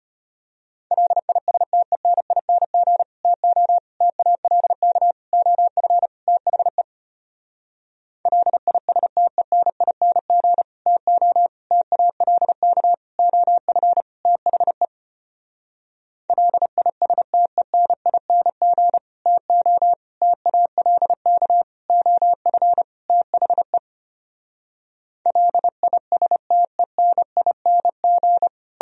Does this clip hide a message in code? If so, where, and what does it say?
Yes: Morse code from 0.9 to 28.5 s, heard as a keyed tone, about 33 wpm, 700 Hz. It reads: LISTENING TO TALK OF THE LISTENING TO TALK OF THE LISTENING TO TALK OF THE LISTENING